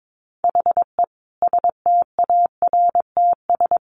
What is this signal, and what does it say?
Morse code, heard as a keyed tone, about 22 wpm, 700 Hz. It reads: HE STARTS
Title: Additional Easy Phrases 22wpm